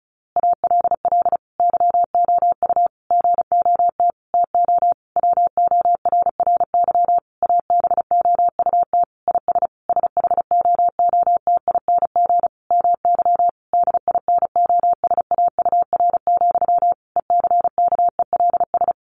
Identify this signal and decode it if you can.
Morse code, heard as a keyed tone, about 34 wpm, 700 Hz. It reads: ALL YOU GOT TO WORRY ABOUT IS SHOOTING MY DINOSAUR, ECKELS
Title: A Sound Of Thunder 35wpm